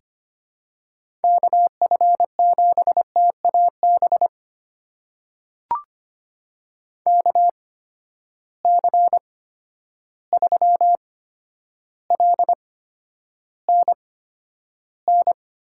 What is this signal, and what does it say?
Morse code, heard as a keyed tone, about 25 wpm, spaced as for 11 wpm, 700 Hz. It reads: KF7TAB KC3LNN
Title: USA Call Signs - 6 Characters - 8x Character Spacing - 25wpm